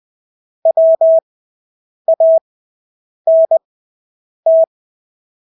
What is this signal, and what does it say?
Morse code, heard as a keyed tone, about 20 wpm, spaced as for 8 wpm, 650 Hz. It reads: WANT
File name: Engelske ord 20/8 wpm